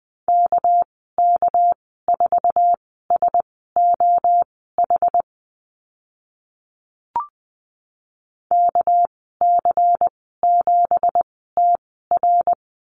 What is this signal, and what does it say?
Morse code, heard as a keyed tone, about 20 wpm, spaced as for 14 wpm, 700 Hz. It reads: KK4SOH KC7TR
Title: 213 Lesson 34g - Number 8 - Call Signs 20wpm